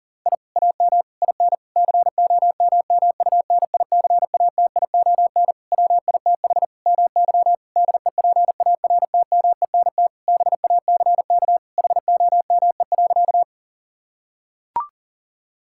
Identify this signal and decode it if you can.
Morse code, heard as a keyed tone, about 40 wpm, 700 Hz. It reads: I AM IN COMMUNICATION WITH MY DEPARTMENT BACK HOME.
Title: Sentences from Top 1000 Words - Review Part 6 - 40wpm